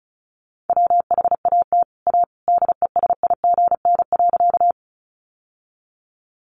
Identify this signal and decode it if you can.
Morse code, heard as a keyed tone, about 34 wpm, 700 Hz. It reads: WHAT A DESIGN.
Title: Sentences from Top 800 Words 35wpm